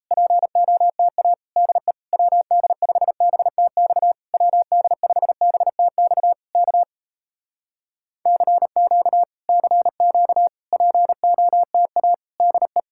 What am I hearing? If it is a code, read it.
Morse code, heard as a keyed tone, about 38 wpm, 700 Hz. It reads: POTA DE WD5BTX WD5BTX K CQ CQ POTA DE